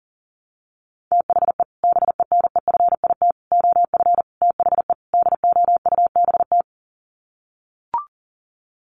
Morse code, heard as a keyed tone, about 40 wpm, 700 Hz. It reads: THE BENEFIT OF THE DOUBT
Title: Sets of 5 Words Encore 40wpm